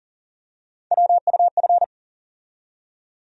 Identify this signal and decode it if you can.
Morse code, heard as a keyed tone, about 40 wpm, 700 Hz. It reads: WUF
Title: USA Call Sign Postfixes - 3 Letters 40wpm